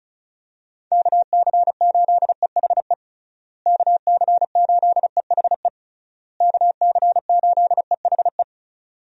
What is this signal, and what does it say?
Morse code, heard as a keyed tone, about 34 wpm, 700 Hz. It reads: KC8EHE KC8EHE KC8EHE
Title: USA Call Signs - 6 Characters - Repeated 3x - 35wpm